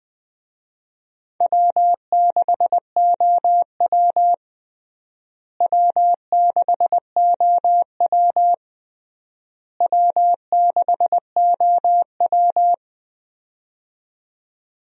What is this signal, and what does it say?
Morse code, heard as a keyed tone, about 20 wpm, 700 Hz. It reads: W6OW W6OW W6OW
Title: USA Call Signs - 4 Characters - Repeated 3x - 20wpm